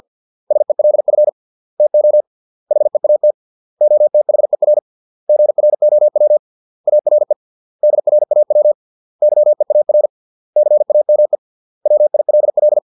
Morse code, heard as a keyed tone, about 50 wpm, 600 Hz. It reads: SELF TO HEAT OTHER GROW ARE DRAW YEAR CAME WILL